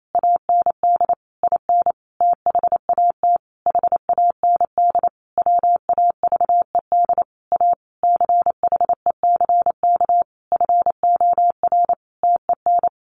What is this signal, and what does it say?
Morse code, heard as a keyed tone, about 28 wpm, 700 Hz. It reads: AND IN THAT HAND WAVED A CHECK FOR TEN
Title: A Sound Of Thunder 28wpm